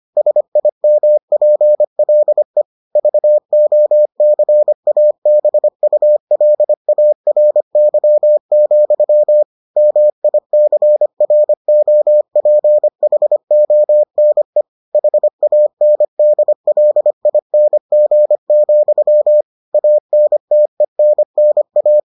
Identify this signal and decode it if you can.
Morse code, heard as a keyed tone, about 25 wpm, 600 Hz. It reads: SIMPLE VOCABULARY, MICROPHONE HANDLING, ANTENNA